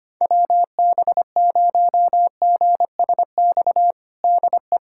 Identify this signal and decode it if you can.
Morse code, heard as a keyed tone, about 25 wpm, 700 Hz. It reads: WB0GSX DE
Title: USA Callsign DE Callsign - 6 Characters - 25wpm